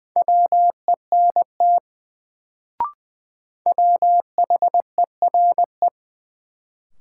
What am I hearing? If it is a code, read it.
Morse code, heard as a keyed tone, about 20 wpm, 700 Hz. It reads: WENT WHERE
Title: Sets of 2 Words - New Word Drill - 20wpm